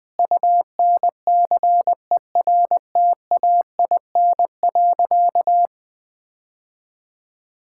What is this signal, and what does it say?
Morse code, heard as a keyed tone, about 21 wpm, 700 Hz. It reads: UNCERTAIN.
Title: Sentences with 3-Word Phrases - 20wpm